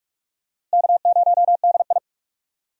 Morse code, heard as a keyed tone, about 43 wpm, 700 Hz. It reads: K0DI
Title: USA Call Signs - 4 Characters - 45wpm